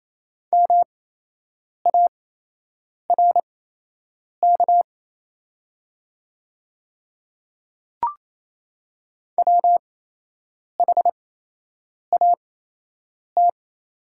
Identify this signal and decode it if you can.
Morse code, heard as a keyed tone, about 28 wpm, spaced as for 8 wpm, 700 Hz. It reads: MARK WHAT
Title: 8x Character Spacing - Top 300 Words - 28wpm